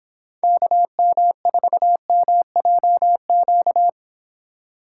Morse code, heard as a keyed tone, about 26 wpm, 700 Hz. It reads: KM4MJQ